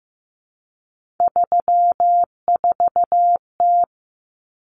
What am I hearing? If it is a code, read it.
Morse code, heard as a keyed tone, about 14 wpm, 700 Hz. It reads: 34T
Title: Common Cut Numbers - 3-Digit 15wpm